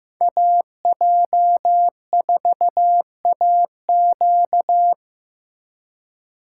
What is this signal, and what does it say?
Morse code, heard as a keyed tone, about 18 wpm, 700 Hz. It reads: AJ4AQ
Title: Speed Racing - USA 2x2 Call Signs 15wpm